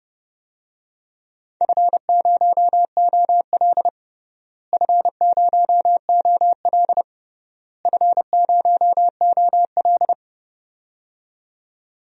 Morse code, heard as a keyed tone, about 32 wpm, 700 Hz. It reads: F0OL F0OL F0OL
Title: DXCC-100 Call Signs - 4 Characters - Repeated 3x - 30wpm